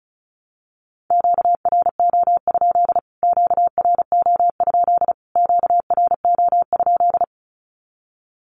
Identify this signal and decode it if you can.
Morse code, heard as a keyed tone, about 34 wpm, 700 Hz. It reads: QRO? QRO? QRO?
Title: Q-Codes Repeated 3x 35wpm